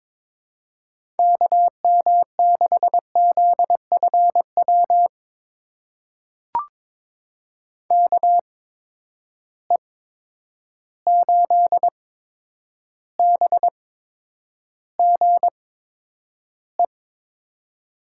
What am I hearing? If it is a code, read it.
Morse code, heard as a keyed tone, about 22 wpm, spaced as for 9 wpm, 700 Hz. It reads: KM6ZFW KE8BGE
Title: USA Call Signs - 6 Characters - 8x Character Spacing - 22wpm